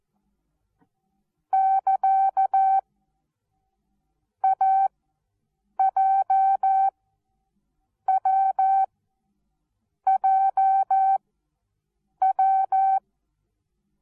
Morse code, heard as a keyed tone, about 13 wpm, 770 Hz. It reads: <KA> A J W J W
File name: Onlinelehrgang / morsen / DJ4UF Morsen19